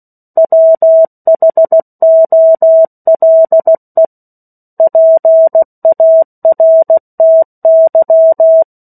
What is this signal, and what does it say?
Morse code, heard as a keyed tone, about 16 wpm, 650 Hz. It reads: WHOLE PARTY
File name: 03-Chapter-03 16 WPM